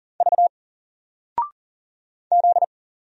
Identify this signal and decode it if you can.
Morse code, heard as a keyed tone, about 40 wpm, 700 Hz. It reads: V Z
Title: Single Character (Letter:Number:Punctuation:Prosign) Alt 40wpm